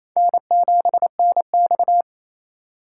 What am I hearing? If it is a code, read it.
Morse code, heard as a keyed tone, about 28 wpm, 700 Hz. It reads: N7NX
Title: USA Call Signs - 4 Characters - 4x Character Spacing - 28wpm